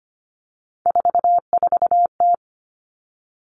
Morse code, heard as a keyed tone, about 25 wpm, 700 Hz. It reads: 44T
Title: Three Challenging Letter-Number Combinations 25wpm